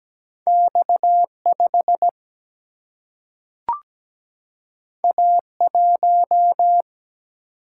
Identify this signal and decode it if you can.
Morse code, heard as a keyed tone, about 17 wpm, 700 Hz. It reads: X5 A1
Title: Single-Letter Single-Number 17wpm